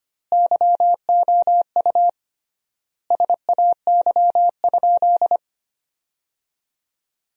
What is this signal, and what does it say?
Morse code, heard as a keyed tone, about 25 wpm, 700 Hz. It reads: YOU SAY?
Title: 3x Word-Spacing - Sentences from Top 400 Words - 25wpm